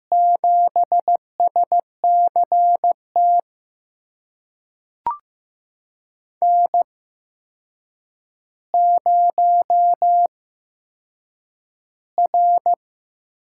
Morse code, heard as a keyed tone, about 15 wpm, spaced as for 7 wpm, 700 Hz. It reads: 7SCT N0R
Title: USA Call Signs - 5 Characters - 8x Character Spacing - 15wpm